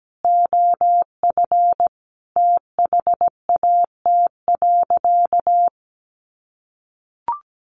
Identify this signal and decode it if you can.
Morse code, heard as a keyed tone, about 17 wpm, 700 Hz. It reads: OF THAT.
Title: Sentences from Top 1200 Words 17wpm